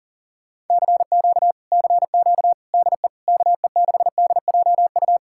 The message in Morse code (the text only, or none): CQ CQ DE KE6DJU